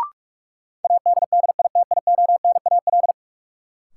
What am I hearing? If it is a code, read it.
Morse code, heard as a keyed tone, about 46 wpm, 700 Hz. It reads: ADDITIONAL
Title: Sentences from Top 1400 Words - New Word Drill Alt - 45wpm